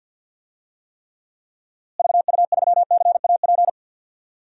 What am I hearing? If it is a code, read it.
Morse code, heard as a keyed tone, about 52 wpm, 700 Hz. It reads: VU3CAP
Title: DXCC-100 Call Signs - 6 Characters - Repeated 3x - 50wpm